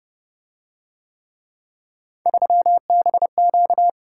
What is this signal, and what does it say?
Morse code, heard as a keyed tone, about 27 wpm, 700 Hz. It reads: 3BQ